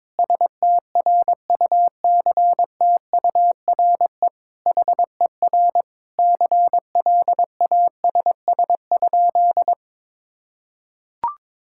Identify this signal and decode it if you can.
Morse code, heard as a keyed tone, about 22 wpm, 700 Hz. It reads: STRUCTURE HER CLASS?
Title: Sentences from Top 1000 Words - Review Part 6 - 22wpm